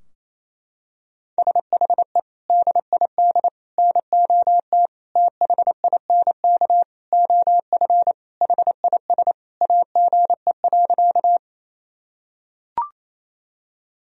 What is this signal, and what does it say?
Morse code, heard as a keyed tone, about 28 wpm, 700 Hz. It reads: SHE DID NOT THINK OF HIS AGE.
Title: Sentences from Top 1000 Words - Review Part 3 - 28wpm